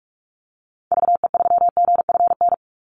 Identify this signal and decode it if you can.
Morse code, heard as a keyed tone, about 44 wpm, 700 Hz. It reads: VE3GFN
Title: Call Signs Repeated 2x - 45wpm